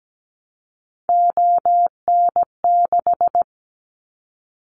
Morse code, heard as a keyed tone, about 17 wpm, 700 Hz. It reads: ON6
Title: Callsign Prefixes for Top 50 DXCC - 17wpm